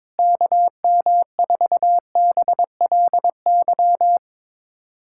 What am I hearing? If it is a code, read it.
Morse code, heard as a keyed tone, about 22 wpm, 700 Hz. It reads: KM4BLY
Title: USA Callsign DE Callsign 22wpm